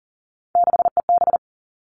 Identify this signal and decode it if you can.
Morse code, heard as a keyed tone, about 42 wpm, 700 Hz. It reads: 6EB
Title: Three Challenging Letter-Number Combinations 40wpm